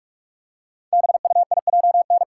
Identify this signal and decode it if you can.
Morse code, heard as a keyed tone, about 44 wpm, 700 Hz. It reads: BUIJN
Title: Code Groups - 5 Letters - 45wpm